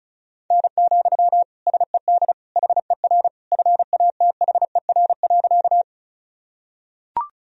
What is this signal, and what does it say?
Morse code, heard as a keyed tone, about 36 wpm, 700 Hz. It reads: N, SED HER FATHER.
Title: Sentences from Top 300 Words and Abbreviations 35wpm